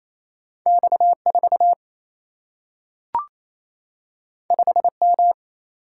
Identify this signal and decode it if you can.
Morse code, heard as a keyed tone, about 28 wpm, 700 Hz. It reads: X 4 5 M
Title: Single-Letter Single-Number 28wpm